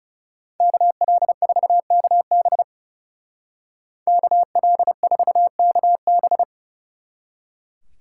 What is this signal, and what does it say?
Morse code, heard as a keyed tone, about 37 wpm, 700 Hz. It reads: KL4KB KL4KB
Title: Speed Racing - USA 2x2 Call Signs 30wpm